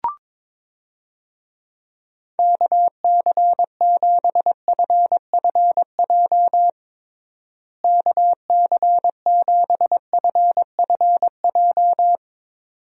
Morse code, heard as a keyed tone, about 22 wpm, 700 Hz. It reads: KC7FFJ KC7FFJ
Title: USA Call Signs - 6 Characters - Repeated 2x - 22wpm